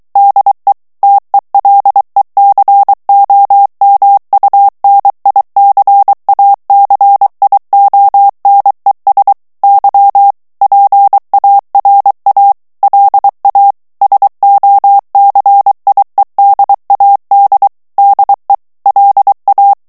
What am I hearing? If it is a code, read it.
Morse code, heard as a keyed tone, about 23 wpm, 800 Hz. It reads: DE TELECOMUNICACIONES Y PARA LA SOCIEDAD DE LA
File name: morse mp3 / ESPAGNOL TEXTE EN CLAIR 2 VITESSE 22 MOTS MINUTES